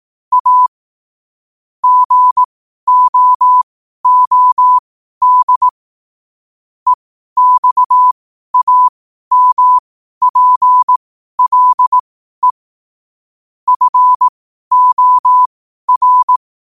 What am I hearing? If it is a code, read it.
Morse code, heard as a keyed tone, about 18 wpm, spaced as for 12 wpm, 1000 Hz. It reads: A GOOD EXAMPLE FOR
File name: qotdfiles / 12wpm